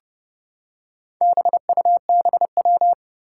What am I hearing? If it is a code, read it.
Morse code, heard as a keyed tone, about 30 wpm, 700 Hz. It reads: BUBW
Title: Four Challenging Letter-Number Combinations 30wpm